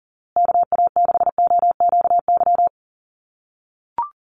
Morse code, heard as a keyed tone, about 40 wpm, 700 Hz. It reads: KA6OQY
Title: USA Call Signs Repeated 2x 40wpm